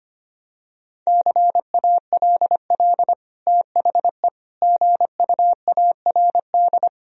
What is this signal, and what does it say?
Morse code, heard as a keyed tone, about 25 wpm, 700 Hz. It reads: CALL THE GUARD
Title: Sentences from Top 1000 Words - Review Part 4 - 25wpm